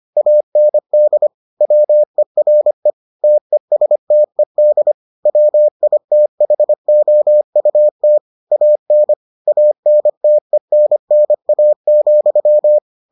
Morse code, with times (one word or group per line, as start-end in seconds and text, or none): 0.2-1.3 AND
1.6-2.9 WERE
3.2-4.9 TESTED
5.3-8.2 WITHOUT
8.5-9.1 AN
9.5-12.8 ANTENNA,